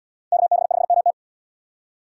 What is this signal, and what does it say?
Morse code, heard as a keyed tone, about 65 wpm, 700 Hz. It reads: KB5MT